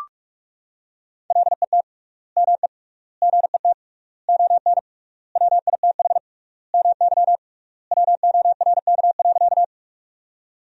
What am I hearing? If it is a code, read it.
Morse code, heard as a keyed tone, about 41 wpm, 700 Hz. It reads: LET ME GET ON WITH MY WORK.
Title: 3x Word-Spacing - Sentences from Top 300 Words - 45wpm